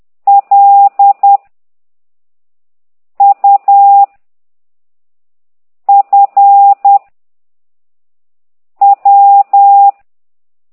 Morse code, heard as a keyed tone, about 10 wpm, 800 Hz. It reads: L U F W
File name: ct2idw / exerc8